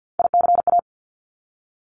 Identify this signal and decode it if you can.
Morse code, heard as a keyed tone, about 50 wpm, 700 Hz. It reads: ICA
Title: Top Three Letter Combinations in Words 50wpm